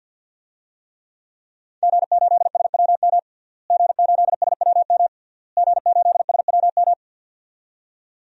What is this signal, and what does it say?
Morse code, heard as a keyed tone, about 52 wpm, 700 Hz. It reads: G8SWM G8SWM G8SWM
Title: DXCC-100 Call Signs - 5 Characters - Repeated 3x - 50wpm